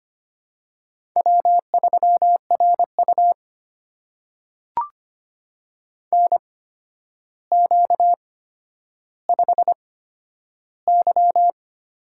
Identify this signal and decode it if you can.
Morse code, heard as a keyed tone, about 25 wpm, spaced as for 11 wpm, 700 Hz. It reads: W3RU NQ5Y
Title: USA Call Signs - 4 Characters - 8x Character Spacing - 25wpm